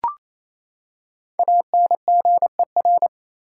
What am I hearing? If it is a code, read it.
Morse code, heard as a keyed tone, about 28 wpm, 700 Hz. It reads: ANGER
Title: Five Letter Words - 28wpm